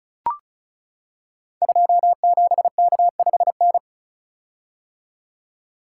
Morse code, heard as a keyed tone, about 36 wpm, 700 Hz. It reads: 27K5N
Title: Code Groups - 5 Letter:Numbers - 35wpm